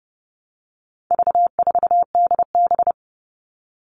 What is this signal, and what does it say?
Morse code, heard as a keyed tone, about 30 wpm, 700 Hz. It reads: V4DB
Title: All Cut Numbers - 4-Digit 30wpm